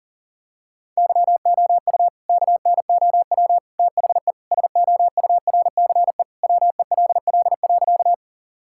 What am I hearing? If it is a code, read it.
Morse code, heard as a keyed tone, about 40 wpm, 700 Hz. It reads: YOU KNOW THE SOURCE WELL.
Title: Sentences from Top 1000 Words - Review Part 5 - 40wpm